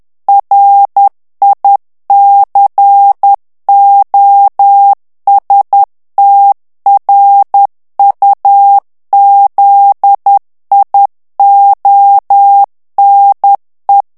Morse code, heard as a keyed tone, about 10 wpm, 800 Hz. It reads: RICOSTRUZIONE
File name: morse mp3 / ITALIEN texte clair 1 vitesse 10 mots minute